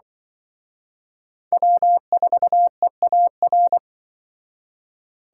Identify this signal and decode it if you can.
Morse code, heard as a keyed tone, about 25 wpm, 700 Hz. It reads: W4EAR
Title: Speed Racing - USA 1x3 Call Signs 20wpm